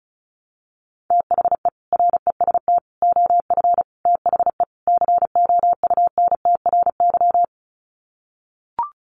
Morse code, heard as a keyed tone, about 34 wpm, 700 Hz. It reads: THE REST OF THE COUNTRY
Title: Sets of 5 Words 35wpm